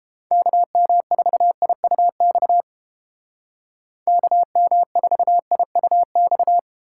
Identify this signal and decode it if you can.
Morse code, heard as a keyed tone, about 32 wpm, 700 Hz. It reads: KM4IUX KM4IUX